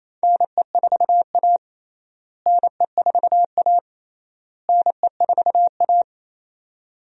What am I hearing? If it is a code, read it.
Morse code, heard as a keyed tone, about 28 wpm, 700 Hz. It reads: NE4A NE4A NE4A
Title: USA Call Signs - 4 Characters - Repeated 3x - 28wpm